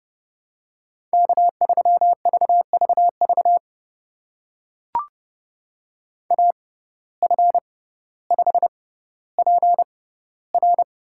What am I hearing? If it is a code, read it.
Morse code, heard as a keyed tone, about 31 wpm, spaced as for 16 wpm, 700 Hz. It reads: K3VVV AF5PR
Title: USA Call Signs - 5 Characters - 6x Character Spacing - 30wpm